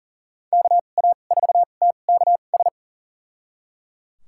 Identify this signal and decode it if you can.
Morse code, heard as a keyed tone, about 37 wpm, spaced as for 28 wpm, 700 Hz. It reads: KA4TKS
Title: USA Call Signs - 2x Character Spacing - 40wpm